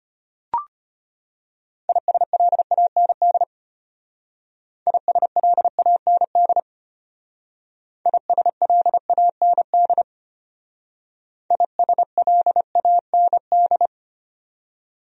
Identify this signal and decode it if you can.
Morse code, heard as a keyed tone, about 31 wpm, 700 Hz. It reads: ISLAND ISLAND ISLAND ISLAND
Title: Speed-Racing Top 500 Words 25wpm